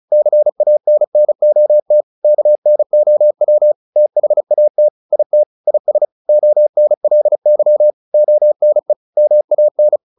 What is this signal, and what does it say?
Morse code, heard as a keyed tone, about 34 wpm, 600 Hz. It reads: CANNOT KNOW THAT IT IS ONLY ONE MAN
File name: the old man and the sea / the old man and the sea 1 35wpm